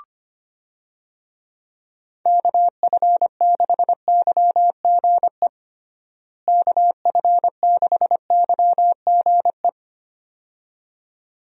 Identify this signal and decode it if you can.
Morse code, heard as a keyed tone, about 25 wpm, 700 Hz. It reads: KF6YGE KF6YGE
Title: USA Call Signs - 6 Characters - Repeated 2x - 25wpm